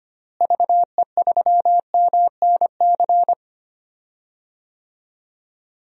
Morse code, heard as a keyed tone, about 25 wpm, 700 Hz. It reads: VE3MNC
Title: DXCC-100 Call Signs - 6 Characters - Repeated 3x - 25wpm